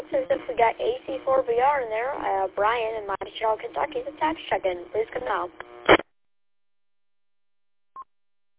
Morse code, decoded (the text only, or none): none